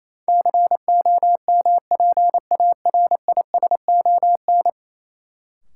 Morse code, heard as a keyed tone, about 28 wpm, 700 Hz. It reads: COMPARISON
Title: Sentences from Top 1600 Words - New Word Drill - 28wpm